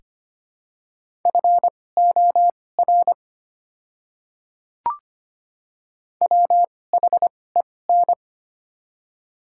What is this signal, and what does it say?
Morse code, heard as a keyed tone, about 25 wpm, spaced as for 18 wpm, 700 Hz. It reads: FOR WHEN
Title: 2x Character Spacing - Top 100 Words - 25wpm